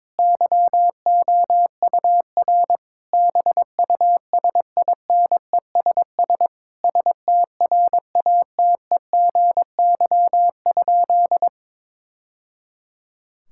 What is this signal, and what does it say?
Morse code, heard as a keyed tone, about 22 wpm, 700 Hz. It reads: YOUR BUSINESS STRATEGY?
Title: Sentences from Top 1000 Words - Review Part 5 - 22wpm